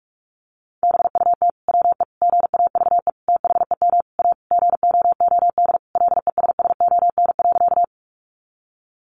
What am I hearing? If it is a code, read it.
Morse code, heard as a keyed tone, about 44 wpm, 700 Hz. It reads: BUT WE GAVE THEM A GOOD LESSON.
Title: Sentences from Top 1200 Words 45wpm